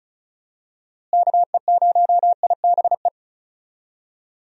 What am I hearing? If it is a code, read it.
Morse code, heard as a keyed tone, about 36 wpm, 700 Hz. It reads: KE0IBE
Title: USA Call Signs - 6 Characters - Repeated 3x - 35wpm